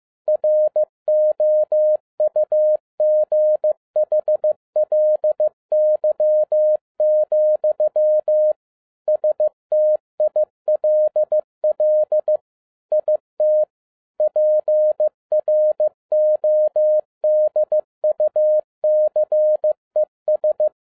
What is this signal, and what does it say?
Morse code, heard as a keyed tone, about 15 wpm, 610 Hz. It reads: ROUGHLY, STILL IT PRODUCES